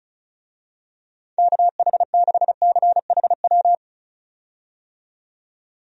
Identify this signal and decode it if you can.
Morse code, heard as a keyed tone, about 34 wpm, 700 Hz. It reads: KH6CHW
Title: USA Call Signs - 6 Characters - 35wpm